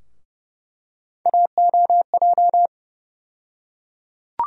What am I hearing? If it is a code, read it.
Morse code, heard as a keyed tone, about 31 wpm, 700 Hz. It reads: AOJ
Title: USA Call Sign Postfixes - 3 Letters 30wpm